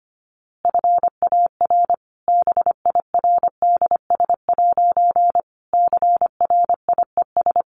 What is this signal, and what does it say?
Morse code, heard as a keyed tone, about 25 wpm, 700 Hz. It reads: FAR BIRDS' CRIES
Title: A Sound Of Thunder 25wpm